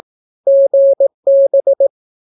Morse code, heard as a keyed tone, about 18 wpm, 550 Hz. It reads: GB
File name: naukapodstaw / skroty / skroty